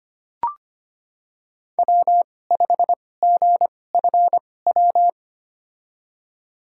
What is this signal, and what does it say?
Morse code, heard as a keyed tone, about 25 wpm, spaced as for 18 wpm, 700 Hz. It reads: W5GFW
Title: USA Call Signs - 5 Characters - 2x Character Spacing - 25wpm